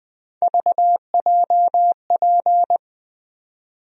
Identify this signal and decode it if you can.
Morse code, heard as a keyed tone, about 20 wpm, 700 Hz. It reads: VJP
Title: USA Call Sign Postfixes - 3 Letters 20wpm